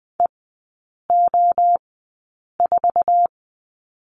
Morse code, heard as a keyed tone, about 20 wpm, spaced as for 8 wpm, 700 Hz. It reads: EO4
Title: 047 Lesson 9e - Number 4 - Three Characters 20wpm